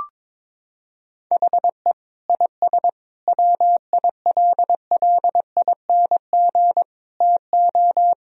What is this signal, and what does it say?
Morse code, heard as a keyed tone, about 22 wpm, 700 Hz. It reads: HE IS WILLING TO
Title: Sentences from Top 1000 Words - Review Part 4 - 22wpm